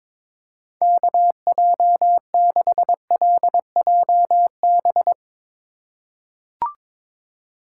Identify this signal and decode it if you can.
Morse code, heard as a keyed tone, about 22 wpm, 700 Hz. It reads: KJ6LJB
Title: USA Call Signs - 6 Characters - 4x Character Spacing - 22wpm